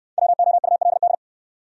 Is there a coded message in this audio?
Yes: Morse code from 0.2 to 1.1 s, heard as a keyed tone, about 63 wpm, 700 Hz. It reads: WOULD